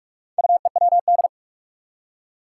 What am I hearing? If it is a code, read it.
Morse code, heard as a keyed tone, about 46 wpm, 700 Hz. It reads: UEWD
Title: Four Challenging Letter-Number Combinations 45wpm